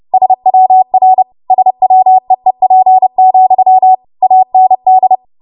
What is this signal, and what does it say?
Morse code, heard as a keyed tone, about 30 wpm, 750 Hz. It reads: SWR SWEEP, AND
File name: Morse / 30WPM